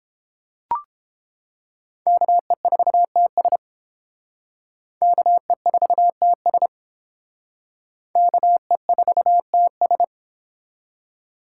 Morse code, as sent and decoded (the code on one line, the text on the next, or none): -.- . ....- - ... / -.- . ....- - ... / -.- . ....- - ...
KE4TS KE4TS KE4TS